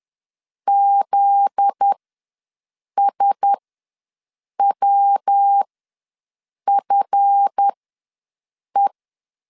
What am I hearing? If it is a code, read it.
Morse code, heard as a keyed tone, about 10 wpm, spaced as for 6 wpm, 800 Hz. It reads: ZSWFE
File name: LEZIONI / 7 GF9Z / 7 GF9Z ws 10wpm 3919